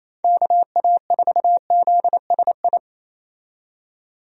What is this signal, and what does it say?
Morse code, heard as a keyed tone, about 28 wpm, 700 Hz. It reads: KA4ZSI
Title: USA Call Signs - 6 Characters - 28wpm